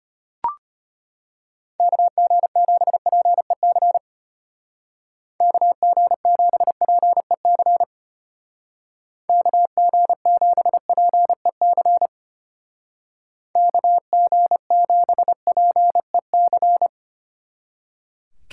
Morse code, heard as a keyed tone, about 33 wpm, 700 Hz. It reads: KG7PEC KG7PEC KG7PEC KG7PEC